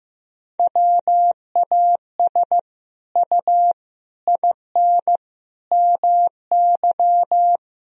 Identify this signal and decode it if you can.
Morse code, heard as a keyed tone, about 15 wpm, 700 Hz. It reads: WAS U IN MY
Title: Sentences from Top 300 Words and Abbreviations 15wpm